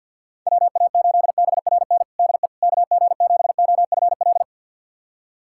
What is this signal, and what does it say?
Morse code, heard as a keyed tone, about 46 wpm, 700 Hz. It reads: WA8BRN DE KG7OFL
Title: USA Callsign DE Callsign - 6 Characters - 50wpm